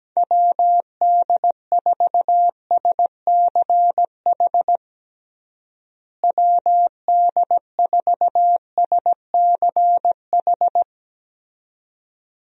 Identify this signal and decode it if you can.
Morse code, heard as a keyed tone, about 17 wpm, 700 Hz. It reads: WD4SCH WD4SCH
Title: USA Call Signs - 6 Characters - Repeated 2x - 17wpm